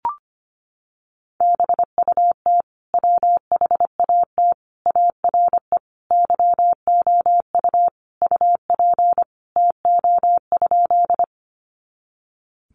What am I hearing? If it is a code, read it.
Morse code, heard as a keyed tone, about 25 wpm, 700 Hz. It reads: BUT WHAT ARE YOU UP TO?
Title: Sentences from Top 100 Words 25wpm